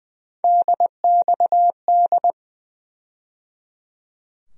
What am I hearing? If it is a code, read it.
Morse code, heard as a keyed tone, about 20 wpm, 700 Hz. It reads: DXD
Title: USA Call Sign Postfixes - 3 Letters 20wpm